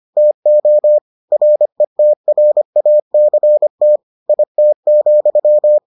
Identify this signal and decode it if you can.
Morse code, heard as a keyed tone, about 25 wpm, 600 Hz. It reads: TO RETRACT IT,